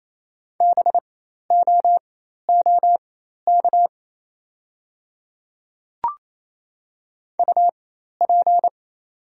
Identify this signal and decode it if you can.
Morse code, heard as a keyed tone, about 28 wpm, spaced as for 13 wpm, 700 Hz. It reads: BOOK UP
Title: 4x Character Spacing - Top 300 Words - 28wpm